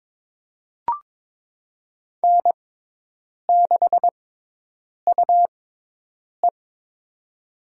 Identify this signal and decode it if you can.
Morse code, heard as a keyed tone, about 22 wpm, spaced as for 8 wpm, 700 Hz. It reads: N6UE
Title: USA Call Signs - 6x Character Spacing - 22wpm